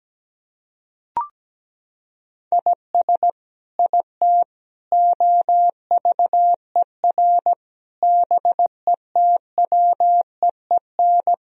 Morse code, heard as a keyed tone, about 17 wpm, 700 Hz. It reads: IS IT OVER BETWEEN
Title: Sentences from Top 1000 Words - Review Part 1 - 17wpm